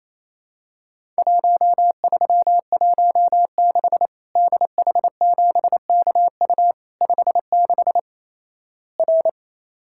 Morse code, heard as a keyed tone, about 27 wpm, 700 Hz. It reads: 1316 DH7KU 56 R
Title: Worked All Europe DX Contest - QTC Exchange - 28wpm